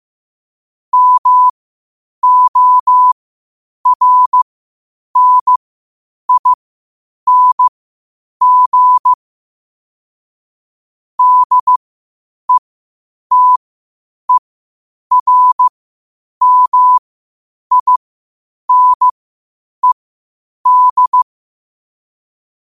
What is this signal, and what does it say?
Morse code, heard as a keyed tone, about 15 wpm, spaced as for 8 wpm, 1000 Hz. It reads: MORNING DETERMINED
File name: qotdfiles / 8wpm